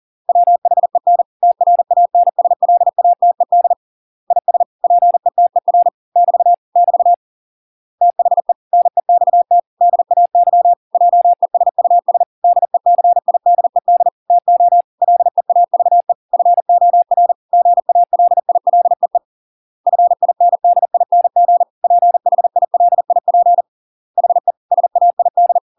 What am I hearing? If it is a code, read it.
Morse code, heard as a keyed tone, about 40 wpm, 700 Hz. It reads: WHEN TRANSLATED IS PETER <BT> <BT> THE NEXT DAY JESUS DECIDED TO LEAVE FOR GALILEE FINDING PHILIP HE SAID